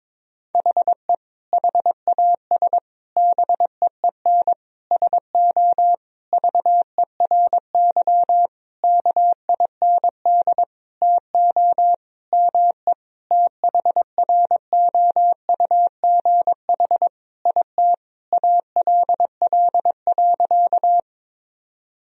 Morse code, heard as a keyed tone, about 22 wpm, 700 Hz. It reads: HE HAS BEEN SO VERY KIND TO ME THROUGH IT ALL.
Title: Sentences from Top 1000 Words - Review Part 1 - 22wpm